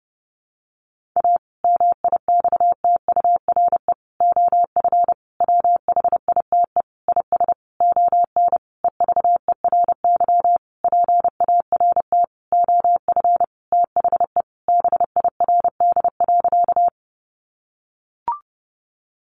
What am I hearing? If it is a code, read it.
Morse code, heard as a keyed tone, about 30 wpm, 700 Hz. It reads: A MIXTURE OF WHITE IS ON EVERY PART OF THE BIRD.
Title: Sentences from Top 1500 Words 30wpm